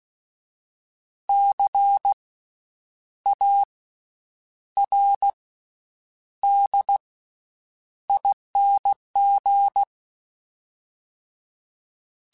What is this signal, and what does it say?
Morse code, heard as a keyed tone, about 16 wpm, spaced as for 8 wpm, 800 Hz. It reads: CARDING
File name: Special INGs / Special INGs-mp3s / VerySlowToFaster Ing @15